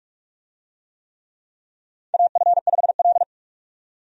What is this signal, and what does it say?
Morse code, heard as a keyed tone, about 47 wpm, 700 Hz. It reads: AF5L
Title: USA Call Signs - 4 Characters - 45wpm